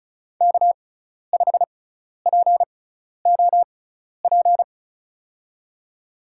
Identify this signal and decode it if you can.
Morse code, heard as a keyed tone, about 36 wpm, spaced as for 12 wpm, 700 Hz. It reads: K5POP
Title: USA Call Signs - 6x Character Spacing - 35wpm